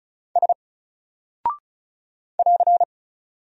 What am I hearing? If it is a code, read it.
Morse code, heard as a keyed tone, about 36 wpm, 700 Hz. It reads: S <AR>
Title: Single Character (Letter:Number:Punctuation:Prosign) Alt 35wpm